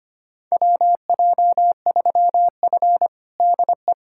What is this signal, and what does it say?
Morse code, heard as a keyed tone, about 25 wpm, 700 Hz. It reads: WJ3F DE
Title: USA Callsign DE Callsign - 4 Characters - 25wpm